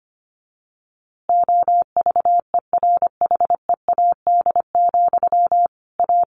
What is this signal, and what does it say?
Morse code, heard as a keyed tone, about 25 wpm, 700 Hz. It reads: OVERHEAD, A
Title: A Sound Of Thunder 25wpm